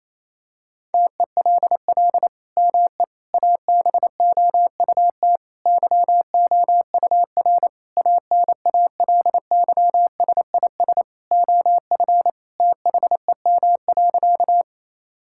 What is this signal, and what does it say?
Morse code, heard as a keyed tone, about 28 wpm, 700 Hz. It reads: TELL ME ABOUT YOUR ANALYSIS OF THEM.
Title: Sentences from Top 1000 Words - Review Part 5 - 28wpm